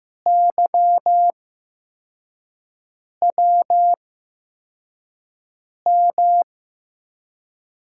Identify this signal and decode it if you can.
Morse code, heard as a keyed tone, about 15 wpm, spaced as for 4 wpm, 700 Hz. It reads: YWM
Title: USA Call Signs - 8x Character Spacing - 15wpm